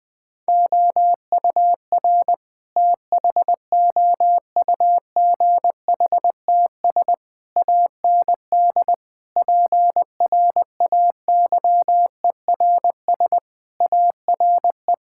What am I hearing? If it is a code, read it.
Morse code, heard as a keyed tone, about 20 wpm, 700 Hz. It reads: OUR THOUGHTS AND PRAYERS ARE
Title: Sentences with 7-Word Phrases - 20wpm